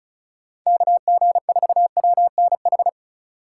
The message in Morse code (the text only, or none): KG4WNH